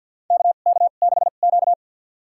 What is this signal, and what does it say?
Morse code, heard as a keyed tone, about 52 wpm, spaced as for 37 wpm, 700 Hz. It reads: KK6Q
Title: USA Call Signs - 4 Characters - 2x Character Spacing - 50wpm